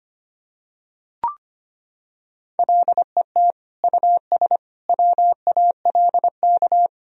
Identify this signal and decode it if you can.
Morse code, heard as a keyed tone, about 25 wpm, 700 Hz. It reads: LET US WALK
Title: Sentences from Top 1000 Words - Review Part 3 - 25wpm